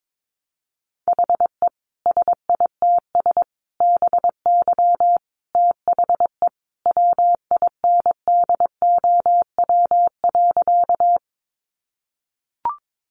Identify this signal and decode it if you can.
Morse code, heard as a keyed tone, about 22 wpm, 700 Hz. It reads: HE SITS BY THE WINDOW.
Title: Additional Easy Phrases 22wpm